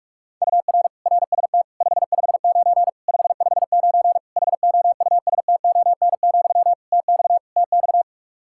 Morse code, heard as a keyed tone, about 43 wpm, 700 Hz. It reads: UR RST 559 559 HOUSTON, TX TX
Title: K1USN SST QSO Practice 45wpm